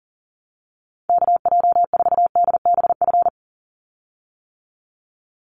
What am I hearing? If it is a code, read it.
Morse code, heard as a keyed tone, about 40 wpm, 700 Hz. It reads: KJ4DBF
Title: USA Call Signs 40wpm